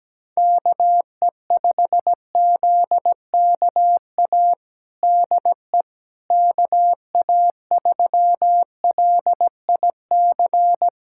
Morse code, heard as a keyed tone, about 17 wpm, 700 Hz. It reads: KE5ZKA DE KA3LIC
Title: USA Callsign DE Callsign 17wpm